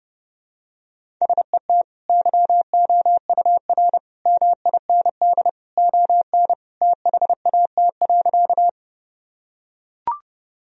Morse code, heard as a keyed tone, about 32 wpm, 700 Hz. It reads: SET YOUR MIND ON THAT.
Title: Sentences from Top 1000 Words - Review Part 1 - 30wpm